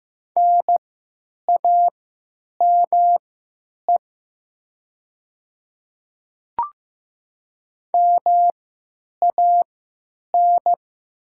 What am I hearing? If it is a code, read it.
Morse code, heard as a keyed tone, about 15 wpm, spaced as for 9 wpm, 700 Hz. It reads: NAME MAN
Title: 3x Character Spacing - Top 500 Words - 15wpm